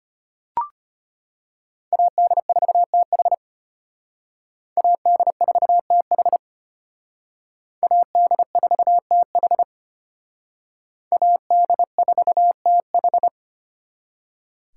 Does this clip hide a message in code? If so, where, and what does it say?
Yes: Morse code from 1.9 to 13.3 s, heard as a keyed tone, about 33 wpm, 700 Hz. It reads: AD4TH AD4TH AD4TH AD4TH